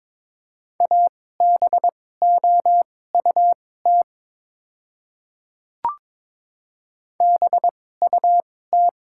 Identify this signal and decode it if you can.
Morse code, heard as a keyed tone, about 22 wpm, spaced as for 16 wpm, 700 Hz. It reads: ABOUT BUT
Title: 2x Character Spacing - Top 100 Words - 22wpm